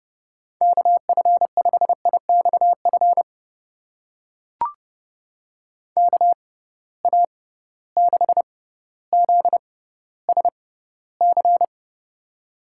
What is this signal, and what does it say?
Morse code, heard as a keyed tone, about 30 wpm, spaced as for 15 wpm, 700 Hz. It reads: KF5IXF KA6ZSC
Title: USA Call Signs - 6 Characters - 6x Character Spacing - 30wpm